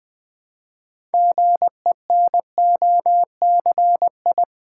Morse code, heard as a keyed tone, about 20 wpm, 700 Hz. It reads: GENOCI
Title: Build A Word - 2 Letters At A Time - 8-Letter Words - 20wpm